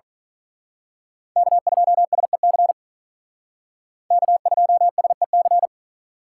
Morse code, heard as a keyed tone, about 41 wpm, 700 Hz. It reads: K2SEC K2SEC
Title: Speed Racing - USA Call Signs 35wpm